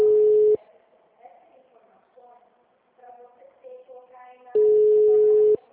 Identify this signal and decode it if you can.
Ringback tone (Europe, 425 Hz), with speech.